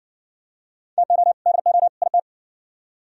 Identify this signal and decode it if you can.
Morse code, heard as a keyed tone, about 55 wpm, 700 Hz. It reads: TO DO IT